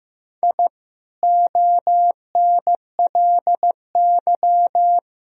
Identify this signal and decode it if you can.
Morse code, heard as a keyed tone, about 15 wpm, 700 Hz. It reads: I ONLY
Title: Sentences from Top 1000 Words - Review Part 1 - 15wpm